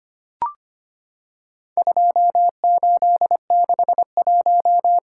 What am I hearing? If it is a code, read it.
Morse code, heard as a keyed tone, about 25 wpm, 700 Hz. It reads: 2861
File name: practice / 25wpm / Four Letter-Number Combinations - 25wpm